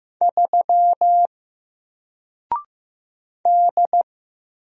Morse code, heard as a keyed tone, about 15 wpm, 700 Hz. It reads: 3 D
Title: Single Character (Letter:Number:Punctuation:Prosign) Alt 15wpm